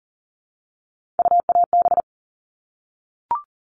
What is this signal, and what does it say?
Morse code, heard as a keyed tone, about 40 wpm, 700 Hz. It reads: UAB
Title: All Cut Numbers - 3-Digit 40wpm